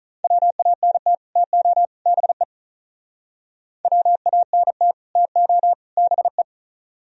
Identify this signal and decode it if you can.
Morse code, heard as a keyed tone, about 45 wpm, 700 Hz. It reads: WANT TO BE WANT TO BE